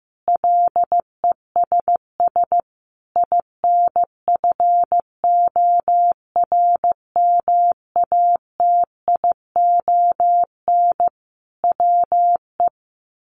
Morse code, heard as a keyed tone, about 15 wpm, 700 Hz. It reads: LESS INFORMATION WE
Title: Sentences from Top 1300 Words 15wpm